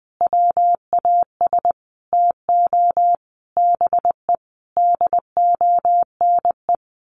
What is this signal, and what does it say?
Morse code, heard as a keyed tone, about 20 wpm, 700 Hz. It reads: WAS TO BE DONE
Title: Sentences from Top 1200 Words 20wpm